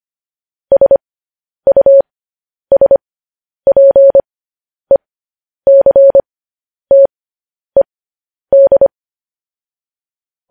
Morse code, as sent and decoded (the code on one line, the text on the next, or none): ... ..- ... .--. -.-. - . -..
SUSPECTED